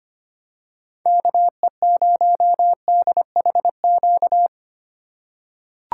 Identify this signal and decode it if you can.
Morse code, heard as a keyed tone, about 25 wpm, 700 Hz. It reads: KE0DHQ
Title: USA Call Signs - 6 Characters - 2x Character Spacing - 25wpm